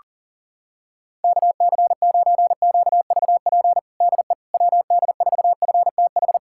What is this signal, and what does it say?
Morse code, heard as a keyed tone, about 40 wpm, 700 Hz. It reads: KC9QVP DE WD4FTH
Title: USA Callsign DE Callsign - 6 Characters - 40wpm